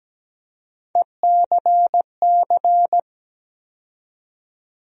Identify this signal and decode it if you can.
Morse code, heard as a keyed tone, about 17 wpm, 700 Hz. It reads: ECC
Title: USA Call Sign Postfixes - 3 Letters 17wpm